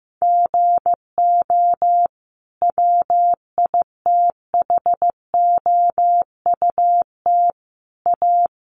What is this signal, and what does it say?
Morse code, heard as a keyed tone, about 15 wpm, 700 Hz. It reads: GO WITHOUT A
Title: Sentences from Top 1300 Words 15wpm